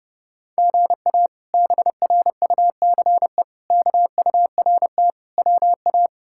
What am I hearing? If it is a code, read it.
Morse code, heard as a keyed tone, about 30 wpm, 700 Hz. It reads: GA BRUCE KURT WA
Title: K1USN SST QSO Practice - 30wpm